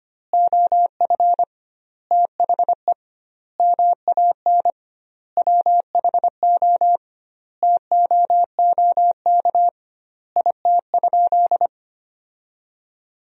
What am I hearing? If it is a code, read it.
Morse code, heard as a keyed tone, about 25 wpm, 700 Hz. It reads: OF THE MAN WHO TOOK IT?
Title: 2x Word-Spacing - Sentences from Top 300 Words - 25wpm